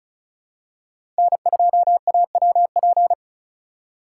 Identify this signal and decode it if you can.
Morse code, heard as a keyed tone, about 36 wpm, 700 Hz. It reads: N2AWP
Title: USA Call Signs - 5 Characters - 2x Character Spacing - 35wpm